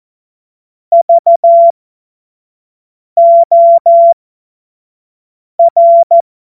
Morse code, heard as a keyed tone, about 14 wpm, spaced as for 5 wpm, 680 Hz. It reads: VOR
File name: Morsekurs / Witze / Witz 28. März 2023 25 F70